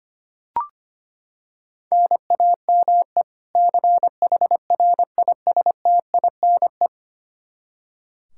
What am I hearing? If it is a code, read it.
Morse code, heard as a keyed tone, about 25 wpm, 700 Hz. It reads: NAME CHRISTINE
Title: Practice QSO - Exchange 1 - Beginning - 25wpm